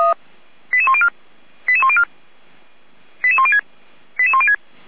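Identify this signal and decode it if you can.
Five-tone selective call (ZVEI), heard as quick runs of five tones, tones of about 70 ms. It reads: 88104 88104 88106 88106
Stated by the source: ZVEI five-tone selective call received by SDR.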